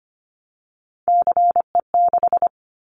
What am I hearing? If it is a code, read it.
Morse code, heard as a keyed tone, about 25 wpm, 700 Hz. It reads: CE6
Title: Callsign Prefixes for Top 100 DXCC - 25wpm